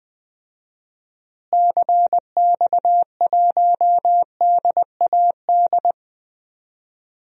Morse code, heard as a keyed tone, about 20 wpm, 700 Hz. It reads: CX1DAD
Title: DXCC-100 Call Signs - 6 Characters - Repeated 2x - 20wpm